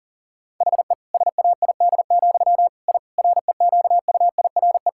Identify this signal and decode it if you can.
Morse code, heard as a keyed tone, about 37 wpm, 700 Hz. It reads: HE SAID, I REQUIRE
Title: Sentences from Top 1000 Words - Review Part 6 - 40wpm